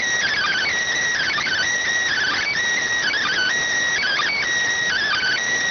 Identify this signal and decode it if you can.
Emergency Alert System (SAME) tones (header bursts), heard as harsh buzzing bursts. It is heard from 1.2 to 5.4 s.